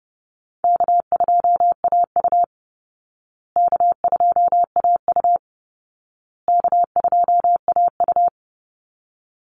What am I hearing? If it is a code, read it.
Morse code, heard as a keyed tone, about 30 wpm, 700 Hz. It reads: K2AU K2AU K2AU
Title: Easy Call Signs Repeated 3x - 30wpm